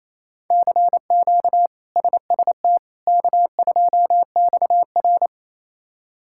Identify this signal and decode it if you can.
Morse code, heard as a keyed tone, about 28 wpm, 700 Hz. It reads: CQ SST K2XR
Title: K1USN SST QSO Practice - 28wpm